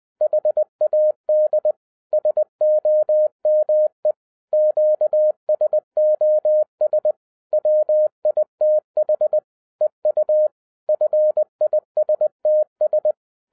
Morse code, heard as a keyed tone, about 20 wpm, 610 Hz. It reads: HAD SOME QSOS WITH EU FISTS